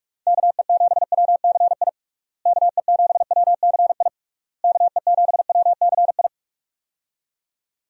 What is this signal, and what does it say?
Morse code, heard as a keyed tone, about 41 wpm, 700 Hz. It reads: KE7WCI KE7WCI KE7WCI
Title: USA Call Signs - 6 Characters - Repeated 3x - 45wpm